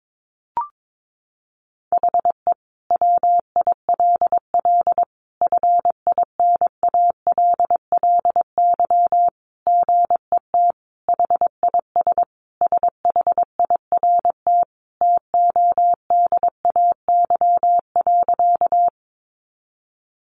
Morse code, heard as a keyed tone, about 22 wpm, 700 Hz. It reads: HE WILL FINALLY GET HIS SHIRT TODAY.
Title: Additional Easy Phrases 22wpm